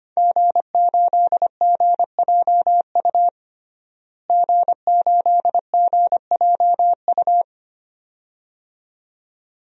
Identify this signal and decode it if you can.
Morse code, heard as a keyed tone, about 25 wpm, 700 Hz. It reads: G8GJU G8GJU
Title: DXCC-100 Call Signs - 5 Characters - Repeated 2x - 25wpm